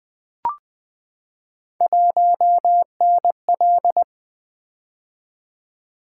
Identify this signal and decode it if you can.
Morse code, heard as a keyed tone, about 20 wpm, 700 Hz. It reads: 1NL
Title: Three Letter-Number Combinations - 20wpm